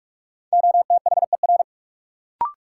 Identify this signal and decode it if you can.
Morse code, heard as a keyed tone, about 45 wpm, 700 Hz. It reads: OTHER
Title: Sentences from Top 100 Words - New Word Drill - 45wpm